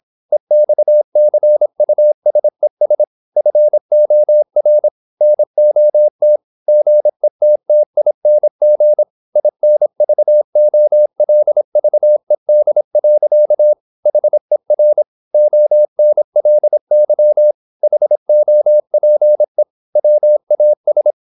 Morse code, heard as a keyed tone, about 26 wpm, 600 Hz. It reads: EXCUSES FOR NOT GETTING INVOLVED. HER ONLY HOPE WAS
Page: Never 26wpm | CW med Gnister